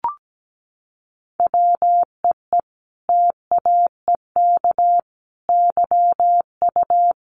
Morse code, heard as a keyed tone, about 17 wpm, 700 Hz. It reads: WEE TAEK YU
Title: A Sound Of Thunder 17wpm